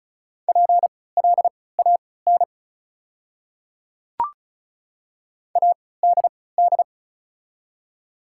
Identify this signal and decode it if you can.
Morse code, heard as a keyed tone, about 34 wpm, spaced as for 20 wpm, 690 Hz. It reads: PLAN ADD
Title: 3x Character Spacing - Top 500 Words - 35wpm